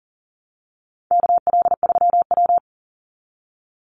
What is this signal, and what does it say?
Morse code, heard as a keyed tone, about 40 wpm, 700 Hz. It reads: KL3W
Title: Call Signs Repeated 2x - 40wpm